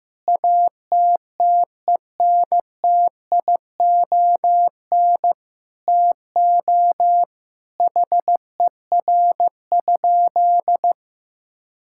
Morse code, heard as a keyed tone, about 16 wpm, 700 Hz. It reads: ATTENTION TO HER?
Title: Sentences from Top 1000 Words - Review Part 6 - 15wpm